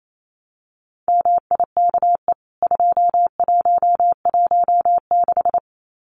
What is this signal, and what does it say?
Morse code, heard as a keyed tone, about 28 wpm, 700 Hz. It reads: MIKE 2116
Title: CWT Exchange 28wpm